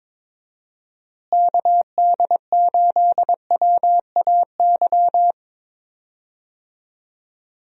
Morse code, heard as a keyed tone, about 21 wpm, 700 Hz. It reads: KD8WAY
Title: USA Call Signs - 6 Characters - Repeated 2x - 22wpm